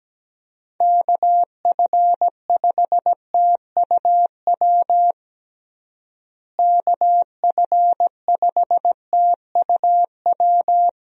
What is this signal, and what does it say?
Morse code, heard as a keyed tone, about 17 wpm, 700 Hz. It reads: KF5TUW KF5TUW
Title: USA Call Signs - 6 Characters - Repeated 3x - 17wpm